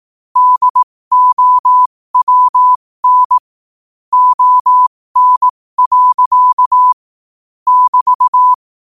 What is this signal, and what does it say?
Morse code, heard as a keyed tone, about 18 wpm, spaced as for 15 wpm, 1000 Hz. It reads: DOWN ON. <BT>
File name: qotdfiles / 15wpm